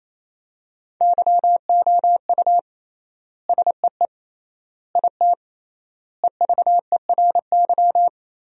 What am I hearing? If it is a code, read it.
Morse code, heard as a keyed tone, about 28 wpm, 700 Hz. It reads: YOU SEE IT EVERY
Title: 3x Word-Spacing - Sentences from Top 200 Words - 28wpm